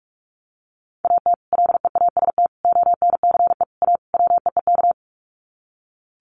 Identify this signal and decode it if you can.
Morse code, heard as a keyed tone, about 44 wpm, 700 Hz. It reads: AT LEAST ONCE A WEEK
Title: Sets of 5 Words Encore2 45wpm